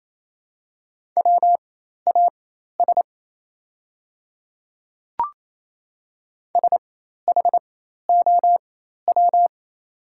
Morse code, heard as a keyed tone, about 28 wpm, spaced as for 13 wpm, 700 Hz. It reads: WAS SHOW
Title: 4x Character Spacing - Top 200 Words - 28wpm